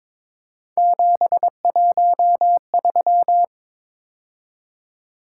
Morse code, heard as a keyed tone, about 23 wpm, 700 Hz. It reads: 713
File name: practice / 22wpm / Three Letter-Number Combinations - 22wpm